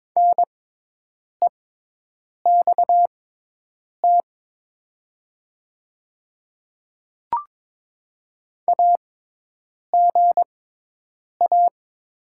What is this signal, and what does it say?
Morse code, heard as a keyed tone, about 22 wpm, spaced as for 8 wpm, 700 Hz. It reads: NEXT AGA
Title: 6x Character Spacing - Top 500 Words - 22wpm